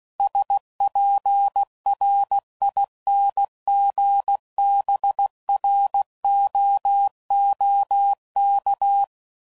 Morse code, heard as a keyed tone, about 16 wpm, 800 Hz. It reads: SPRINGBROOK
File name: CW Code Groups @15WPM